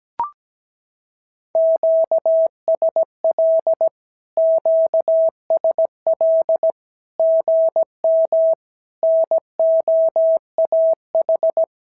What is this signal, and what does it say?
Morse code, heard as a keyed tone, about 17 wpm, 650 Hz. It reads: QSL QSL GM NOAH
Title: SOTA QSO Practice 17wpm